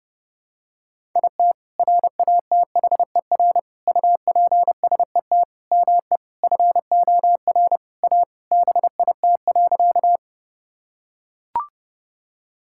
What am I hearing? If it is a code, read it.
Morse code, heard as a keyed tone, about 32 wpm, 700 Hz. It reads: IT RATHER UPSET ME FOR A BIT.
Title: Sentences from Top 1000 Words - Review Part 5 - 30wpm